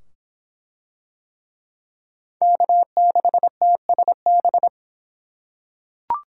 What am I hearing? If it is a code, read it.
Morse code, heard as a keyed tone, about 23 wpm, 700 Hz. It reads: K6TSB